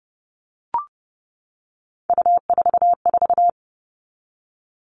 Morse code, heard as a keyed tone, about 30 wpm, 700 Hz. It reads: U44
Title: All Cut Numbers - 3-Digit 30wpm